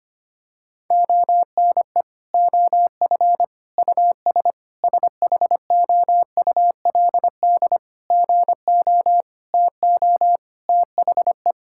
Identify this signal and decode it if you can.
Morse code, heard as a keyed tone, about 25 wpm, 700 Hz. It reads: ONE OF US SHOULD GO TO THE
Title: Sentences from Top 1600 Words 25wpm